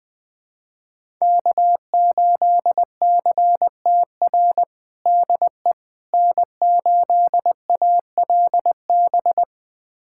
Morse code, heard as a keyed tone, about 20 wpm, 700 Hz. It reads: K8CTR DE N8ALB
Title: USA Callsign DE Callsign - 5 Characters - 20wpm